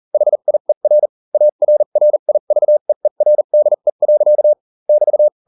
Morse code, heard as a keyed tone, about 40 wpm, spaced as for 35 wpm, 600 Hz. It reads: HIER ARRIVEERDE. <BT>